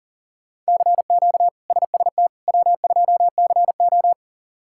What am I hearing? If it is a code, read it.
Morse code, heard as a keyed tone, about 40 wpm, 700 Hz. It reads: CQ SST W2CO
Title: K1USN SST QSO Practice - 40wpm